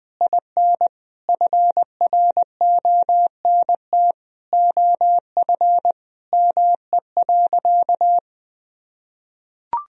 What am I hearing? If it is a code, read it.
Morse code, heard as a keyed tone, about 20 wpm, 700 Hz. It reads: IN FRONT OF ME.
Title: Sentences from Top 1000 Words - Review Part 2 - 20wpm